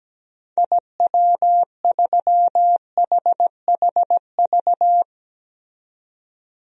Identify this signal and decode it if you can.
Morse code, heard as a keyed tone, about 17 wpm, 700 Hz. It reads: IW3HHV
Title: DXCC-100 Call Signs - 6 Characters - 17wpm